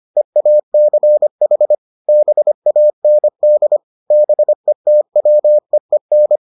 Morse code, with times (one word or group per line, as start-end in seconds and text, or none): 0.2-1.7 EACH
2.1-3.8 BAND
4.1-6.4 BETWEEN